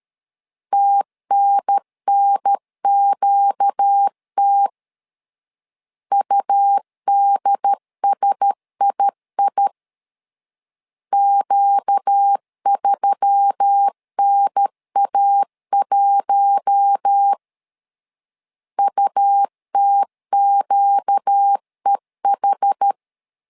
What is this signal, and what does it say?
Morse code, heard as a keyed tone, about 13 wpm, 800 Hz. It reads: TNNQT UDSII Q3NA1 UTQEH
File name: LEZIONI / 5 LBVC / 5 L 12wpm 1913